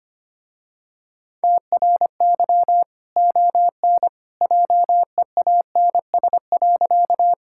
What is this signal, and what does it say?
Morse code, heard as a keyed tone, about 25 wpm, 700 Hz. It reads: TRY ON JEANS.
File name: Easy Sentences with Hints - 25wpm